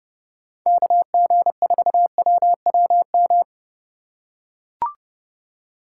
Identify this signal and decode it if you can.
Morse code, heard as a keyed tone, about 28 wpm, 700 Hz. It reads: KG4WWM